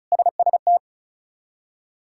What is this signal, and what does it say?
Morse code, heard as a keyed tone, about 34 wpm, 700 Hz. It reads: SST
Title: K1USN SST QSO Practice - 35wpm